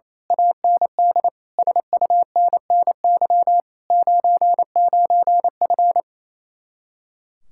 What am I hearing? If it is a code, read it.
Morse code, heard as a keyed tone, about 27 wpm, 700 Hz. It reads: AND SUNNY 99F
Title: Speed-Racing Common QSO Phrases Encore - 28wpm